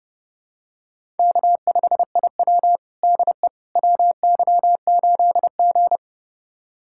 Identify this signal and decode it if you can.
Morse code, heard as a keyed tone, about 30 wpm, 700 Hz. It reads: K5IW DE WY8G
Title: USA Callsign DE Callsign - 4 Characters - 30wpm